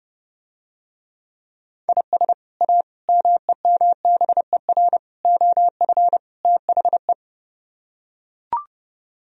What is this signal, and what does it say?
Morse code, heard as a keyed tone, about 30 wpm, 700 Hz. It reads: IS A MEMBER OF THE